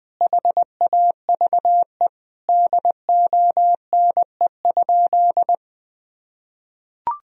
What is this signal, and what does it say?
Morse code, heard as a keyed tone, about 20 wpm, 700 Hz. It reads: HAVE DONE?
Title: Sentences from Top 1000 Words - Review Part 3 - 20wpm